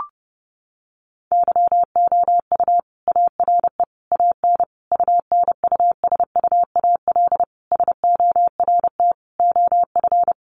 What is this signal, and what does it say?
Morse code, heard as a keyed tone, about 30 wpm, 700 Hz. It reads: YOU ARE AN UNUSUAL SORT OF
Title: Sentences from Top 1400 Words 30wpm